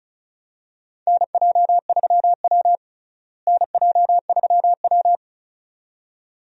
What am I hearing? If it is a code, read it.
Morse code, heard as a keyed tone, about 36 wpm, 700 Hz. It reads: NJ3W NJ3W
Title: USA Call Signs - 4 Characters - Repeated 2x - 35wpm